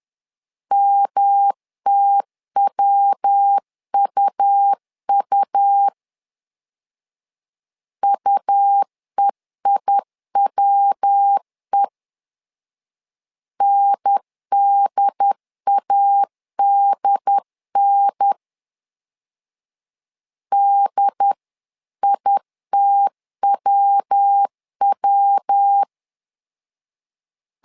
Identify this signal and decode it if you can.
Morse code, heard as a keyed tone, about 10 wpm, 800 Hz. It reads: MTWUU UEIWE NDADN DITWW
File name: LEZIONI / 3 MWRO / 3 MW 10wpm 1959